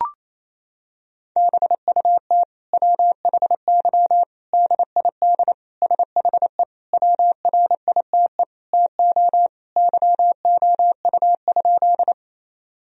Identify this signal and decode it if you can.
Morse code, heard as a keyed tone, about 28 wpm, 700 Hz. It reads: BUT WHY DID SHE WRITE TO YOU?
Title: Sentences from Top 1000 Words - Review Part 1 - 28wpm